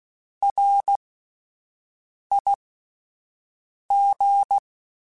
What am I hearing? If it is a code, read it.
Morse code, heard as a keyed tone, about 16 wpm, spaced as for 5 wpm, 800 Hz. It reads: RIG